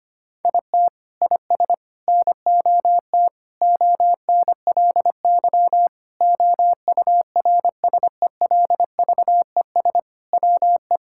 Morse code, heard as a keyed tone, about 25 wpm, 700 Hz. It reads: IT IS NOT ONLY OURSELVES WE
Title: Sentences from Top 1600 Words 25wpm